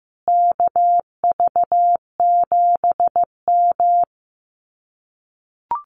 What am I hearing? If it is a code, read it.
Morse code, heard as a keyed tone, about 15 wpm, 700 Hz. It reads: KV7M
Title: Common QSO Phrases Encore 15wpm